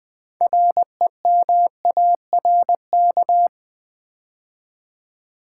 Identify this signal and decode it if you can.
Morse code, heard as a keyed tone, about 20 wpm, 700 Hz. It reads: REMARK
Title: Build A Word - 3 Letters At A Time - 6-Letter Words - 20wpm